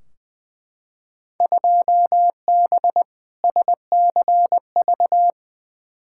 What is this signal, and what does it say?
Morse code, heard as a keyed tone, about 21 wpm, 700 Hz. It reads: 2B SCV
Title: Field Day Exchanges Encore - 20wpm